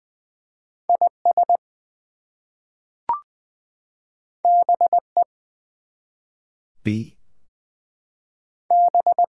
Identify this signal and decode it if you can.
Morse code, heard as a keyed tone, about 20 wpm, 700 Hz. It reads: IS BE B
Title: Two Letter Words - 20wpm